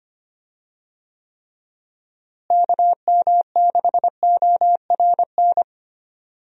Morse code, heard as a keyed tone, about 25 wpm, 700 Hz. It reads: KM6ORN
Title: USA Call Signs - 6 Characters - Repeated 2x - 25wpm